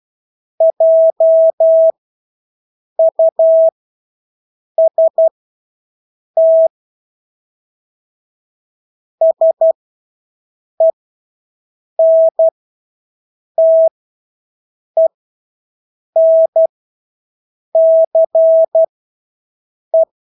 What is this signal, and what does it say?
Morse code, heard as a keyed tone, about 12 wpm, spaced as for 6 wpm, 650 Hz. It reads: JUST SENTENCE
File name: Engelske ord 12-6 wpm